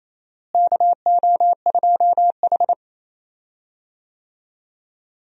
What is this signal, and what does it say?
Morse code, heard as a keyed tone, about 27 wpm, 700 Hz. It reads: KO2H
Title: USA Call Signs - 4 Characters - Repeated 3x - 28wpm